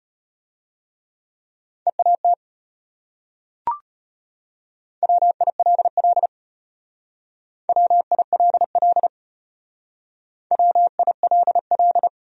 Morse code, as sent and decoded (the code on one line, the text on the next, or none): . .- - / .-- .. .-.. .-.. / .-- .. .-.. .-.. / .-- .. .-.. .-..
EAT WILL WILL WILL